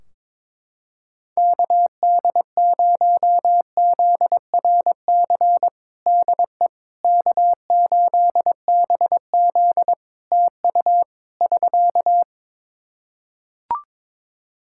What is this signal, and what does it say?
Morse code, heard as a keyed tone, about 21 wpm, 700 Hz. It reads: KD0ZRC DE K8BZ TU <SK>
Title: Practice QSO - Exchange 4 - Ending - 22wpm